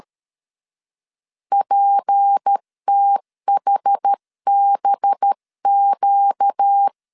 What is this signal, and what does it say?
Morse code, heard as a keyed tone, about 13 wpm, 800 Hz. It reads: PTHBQ
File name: CW / DET / 1 6 1500C S C 12wpm 3533